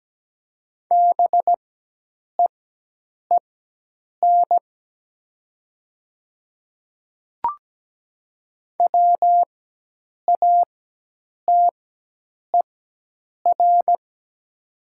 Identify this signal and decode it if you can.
Morse code, heard as a keyed tone, about 17 wpm, spaced as for 8 wpm, 700 Hz. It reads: BEEN WATER
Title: 4x Character Spacing - Top 500 Words - 17wpm